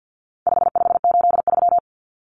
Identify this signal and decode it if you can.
Morse code, heard as a keyed tone, about 50 wpm, 700 Hz. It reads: 5583
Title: Common Cut Numbers - 4-Digit 50wpm